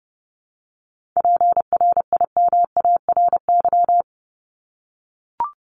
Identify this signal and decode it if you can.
Morse code, heard as a keyed tone, about 30 wpm, 700 Hz. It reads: PRIMARY
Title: LY Suffix - Top 100 Words with Base Words 30wpm